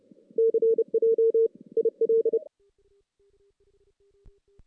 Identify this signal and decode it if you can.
Morse code, heard as a keyed tone, about 30 wpm, 410 Hz. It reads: CJ IL
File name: 2016 JAN CW NAQP / 2016 Jan naqp k5ot - 0-5